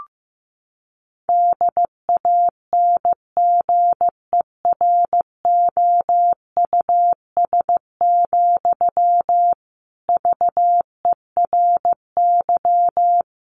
Morse code, heard as a keyed tone, about 15 wpm, 700 Hz. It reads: DANGEROUS, VERY
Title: Sentences from Top 1200 Words 15wpm